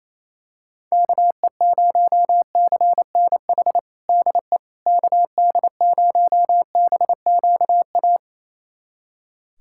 Morse code, heard as a keyed tone, about 28 wpm, 700 Hz. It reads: KE0CNH DE KD0BQA
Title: USA Callsign DE Callsign - 6 Characters - 28wpm